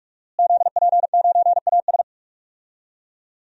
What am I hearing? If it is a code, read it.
Morse code, heard as a keyed tone, about 46 wpm, 700 Hz. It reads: ZP9AS
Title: DXCC-100 Call Signs - 5 Characters - Repeated 3x - 45wpm